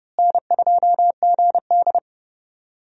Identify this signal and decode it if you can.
Morse code, heard as a keyed tone, about 29 wpm, 700 Hz. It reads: N2GD
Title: USA Call Signs - 4 Characters - Repeated 2x - 30wpm